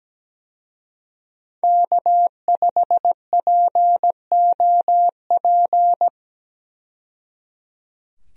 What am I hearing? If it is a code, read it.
Morse code, heard as a keyed tone, about 17 wpm, 700 Hz. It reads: K5POP
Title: USA Call Signs - 5 Characters - 17wpm